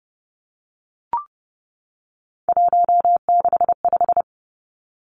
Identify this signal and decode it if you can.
Morse code, heard as a keyed tone, about 30 wpm, 700 Hz. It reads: 165
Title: Three Challenging Letter-Number Combinations 30wpm